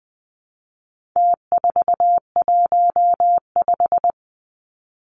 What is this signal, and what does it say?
Morse code, heard as a keyed tone, about 20 wpm, 700 Hz. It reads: T415
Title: Numbers - Contest Style - 4-Digit 20wpm